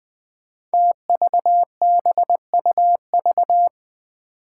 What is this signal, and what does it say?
Morse code, heard as a keyed tone, about 21 wpm, 700 Hz. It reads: TVBUV
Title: Five Challenging Letter-Number Combinations 20wpm